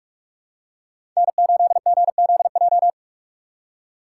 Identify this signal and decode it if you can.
Morse code, heard as a keyed tone, about 46 wpm, 700 Hz. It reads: N8GZJ
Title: USA Call Signs - 5 Characters - 3x Character Spacing - 45wpm